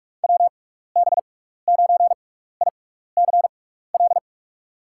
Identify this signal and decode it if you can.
Morse code, heard as a keyed tone, about 44 wpm, spaced as for 15 wpm, 700 Hz. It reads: WB9ICL
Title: USA Call Signs - 6 Characters - 6x Character Spacing - 45wpm